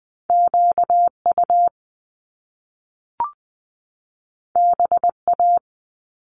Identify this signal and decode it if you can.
Morse code, heard as a keyed tone, about 20 wpm, 700 Hz. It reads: QU BA
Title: Top Two Letter Combinations in Words 20wpm